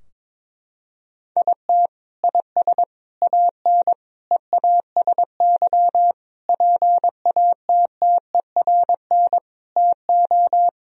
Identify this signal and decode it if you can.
Morse code, heard as a keyed tone, about 22 wpm, 700 Hz. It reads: IT IS AN EASY PATTERN TO
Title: Sentences from Top 1000 Words - Review Part 2 - 22wpm